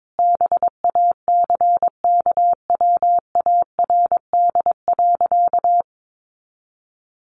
Morse code, heard as a keyed tone, about 22 wpm, 700 Hz. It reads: BACKWARD.
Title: A Sound Of Thunder 22wpm